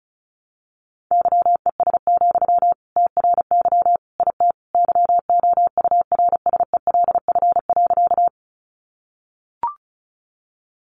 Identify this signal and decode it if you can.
Morse code, heard as a keyed tone, about 34 wpm, 700 Hz. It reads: YES, TRY IT YOURSELF.
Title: Sentences from Top 600 Words 35wpm